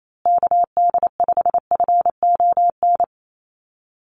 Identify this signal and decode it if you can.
Morse code, heard as a keyed tone, about 28 wpm, 700 Hz. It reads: KD5FON
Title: USA Call Signs Repeated 3x 28wpm